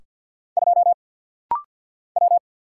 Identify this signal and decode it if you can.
Morse code, heard as a keyed tone, about 50 wpm, 700 Hz. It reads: 2 W
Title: Single Letter-Number Alt - Rapid-Fire 50wpm